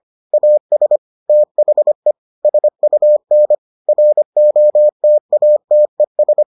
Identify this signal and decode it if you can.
Morse code, heard as a keyed tone, about 25 wpm, 600 Hz. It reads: AS THE SUN ROTATES